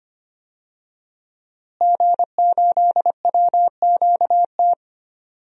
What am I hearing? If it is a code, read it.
Morse code, heard as a keyed tone, about 25 wpm, 700 Hz. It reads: G8WQT
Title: DXCC-100 Call Signs - 5 Characters - Repeated 2x - 25wpm